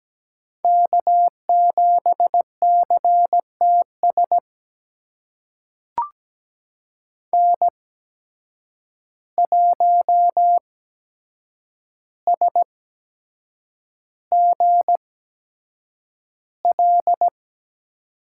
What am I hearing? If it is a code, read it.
Morse code, heard as a keyed tone, about 17 wpm, spaced as for 7 wpm, 700 Hz. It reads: K7CTS N1SGL
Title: USA Call Signs - 5 Characters - 8x Character Spacing - 17wpm